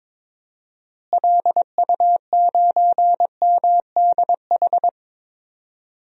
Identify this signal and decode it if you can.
Morse code, heard as a keyed tone, about 22 wpm, 700 Hz. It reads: LU9MDH
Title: DXCC-100 Call Signs - 6 Characters - Repeated 2x - 22wpm